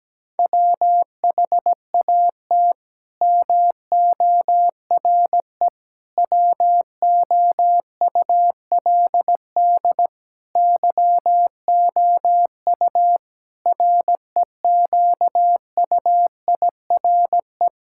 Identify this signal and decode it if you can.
Morse code, heard as a keyed tone, about 17 wpm, 700 Hz. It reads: WHAT MORE WOULD YOU REQUIRE
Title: Sentences from Top 1000 Words - Review Part 6 - 17wpm